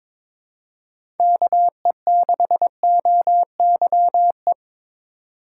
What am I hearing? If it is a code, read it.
Morse code, heard as a keyed tone, about 22 wpm, 700 Hz. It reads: KE6OYE
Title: USA Call Signs - 2x Character Spacing - 22wpm